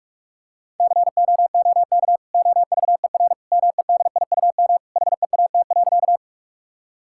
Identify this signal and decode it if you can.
Morse code, heard as a keyed tone, about 46 wpm, 700 Hz. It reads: COOK OVER MEDIUM HEAT.
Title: Sentences from Top 1000 Words - Review Part 6 - 45wpm